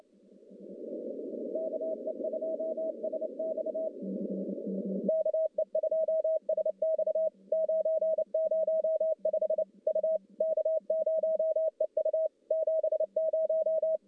Morse code, heard as a keyed tone, about 29 wpm, 640 Hz. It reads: KE2SX KE2SX 905 U K0EU 70
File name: contests / CW-SS-k5tr- -1200